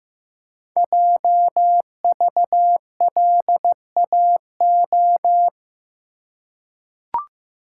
Morse code, heard as a keyed tone, about 15 wpm, 700 Hz. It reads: JVLAO
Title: Code Groups - 5 Letter:Numbers - 15wpm